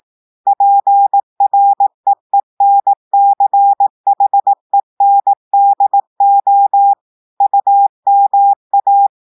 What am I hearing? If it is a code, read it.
Morse code, heard as a keyed tone, about 18 wpm, 800 Hz. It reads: PREENCHENDO UMA
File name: audios / portugais / portugais texte en clair 1 vitesse 18 mots minutes